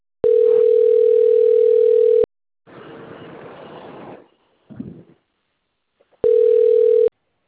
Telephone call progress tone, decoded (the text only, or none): ringback tone